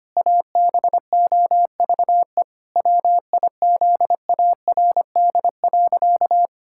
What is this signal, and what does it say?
Morse code, heard as a keyed tone, about 25 wpm, 700 Hz. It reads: ABOVE WIZARD.
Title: Pangram Challenge v2 - 25wpm